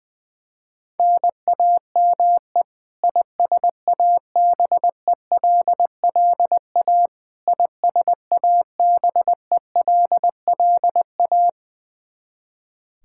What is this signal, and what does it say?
Morse code, heard as a keyed tone, about 21 wpm, 700 Hz. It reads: NAME ISABELLA ISABELLA
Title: Practice QSO - Exchange 1 - Beginning - 20wpm